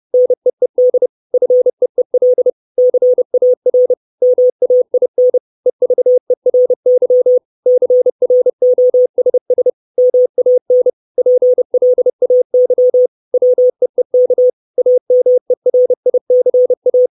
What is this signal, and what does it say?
Morse code, heard as a keyed tone, about 30 wpm, 500 Hz. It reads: NEED FEEL CAR MAIN EVERY CROSS MAN PLAY WEEK AMERICA
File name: 632 Words 30WPM